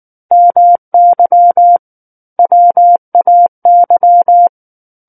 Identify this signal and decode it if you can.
Morse code, heard as a keyed tone, about 19 wpm, 700 Hz. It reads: MY WAY
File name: Book One / 10-Chapter 10 - 19 WPM